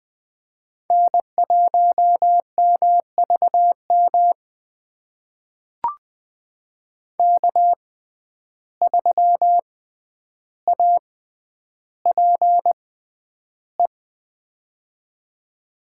Morse code, heard as a keyed tone, about 21 wpm, spaced as for 10 wpm, 700 Hz. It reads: N1MVM K3APE
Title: USA Call Signs - 5 Characters - 6x Character Spacing - 20wpm